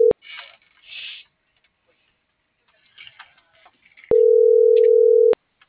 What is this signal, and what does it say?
Ringback tone (North America), 440 + 480 Hz.